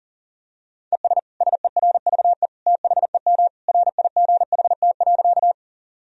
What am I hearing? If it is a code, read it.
Morse code, heard as a keyed tone, about 37 wpm, 700 Hz. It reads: ES SERVE THEM RIGHT.
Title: Sentences from Top 400 Words and Abbreviations 40wpm